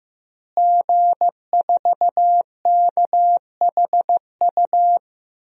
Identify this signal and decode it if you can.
Morse code, heard as a keyed tone, about 15 wpm, 700 Hz. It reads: G4KHU
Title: DXCC-100 Call Signs - 5 Characters - Repeated 3x - 15wpm